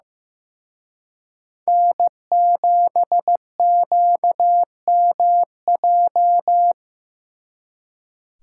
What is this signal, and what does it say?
Morse code, heard as a keyed tone, about 18 wpm, 700 Hz. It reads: N7QMJ
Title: Speed Racing - USA 1x3 Call Signs 15wpm